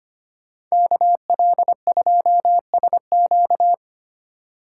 Morse code, heard as a keyed tone, about 29 wpm, 700 Hz. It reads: KL2SQ
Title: Speed Racing - USA 2x2 Call Signs 25wpm